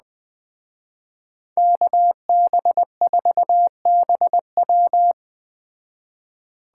Morse code, heard as a keyed tone, about 20 wpm, 700 Hz. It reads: KB4BW
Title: Speed Racing - USA 2x2 Call Signs 17wpm